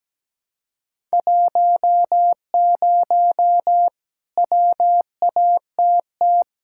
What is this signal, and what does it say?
Morse code, heard as a keyed tone, about 17 wpm, 700 Hz. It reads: 10 WATT
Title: Speed-Racing Common QSO Phrases - 17wpm